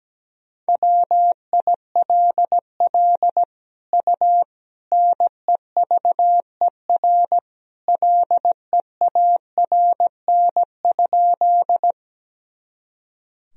Morse code, heard as a keyed tone, about 17 wpm, 700 Hz. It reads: WILL U NEVER LEARN?
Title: Sentences from Top 300 Words and Abbreviations 17wpm